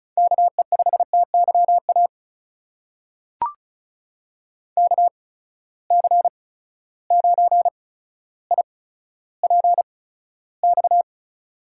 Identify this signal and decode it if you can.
Morse code, heard as a keyed tone, about 36 wpm, spaced as for 15 wpm, 700 Hz. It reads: KE5TYA KC9IPX
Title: USA Call Signs - 6 Characters - 8x Character Spacing - 35wpm